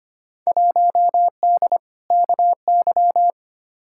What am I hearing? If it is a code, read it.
Morse code, heard as a keyed tone, about 25 wpm, 700 Hz. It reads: 1D KY
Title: Field Day Exchanges - 25wpm